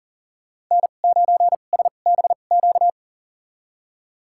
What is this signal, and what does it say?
Morse code, heard as a keyed tone, about 40 wpm, spaced as for 29 wpm, 700 Hz. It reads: N9SBQ
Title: USA Call Signs - 2x Character Spacing - 40wpm